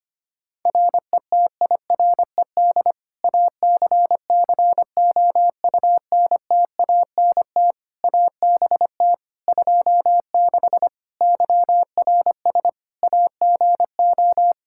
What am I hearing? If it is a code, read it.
Morse code, heard as a keyed tone, about 25 wpm, 700 Hz. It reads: RETIRED ACCOUNTANT ABT 26 YRS AGO
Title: Practice QSO - Exchange 3 - More Info - 25wpm